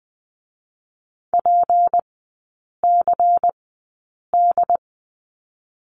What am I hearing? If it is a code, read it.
Morse code, heard as a keyed tone, about 20 wpm, spaced as for 8 wpm, 700 Hz. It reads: PCD
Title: 152 Lesson 25e - Letter p - Three Characters 20wpm